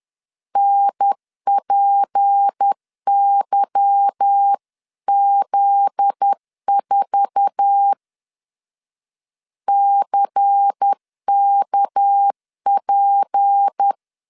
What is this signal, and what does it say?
Morse code, heard as a keyed tone, about 10 wpm, 800 Hz. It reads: NPYZ4 CKP
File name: CW / DET / 1 9 1500C S A 10wpm 4511